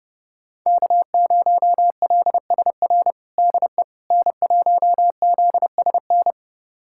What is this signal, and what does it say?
Morse code, heard as a keyed tone, about 31 wpm, 700 Hz. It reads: K0LSR DE N1ZSN
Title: USA Callsign DE Callsign - 5 Characters - 30wpm